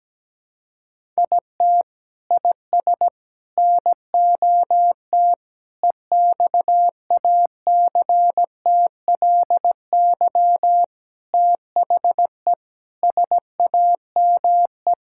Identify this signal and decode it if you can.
Morse code, heard as a keyed tone, about 17 wpm, 700 Hz. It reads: IT IS NOT EXACTLY THE SAME
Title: Sentences from Top 1000 Words - Review Part 6 - 17wpm